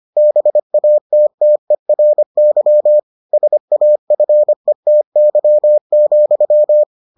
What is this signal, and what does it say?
Morse code, heard as a keyed tone, about 25 wpm, 600 Hz. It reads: BATTERY SAFETY,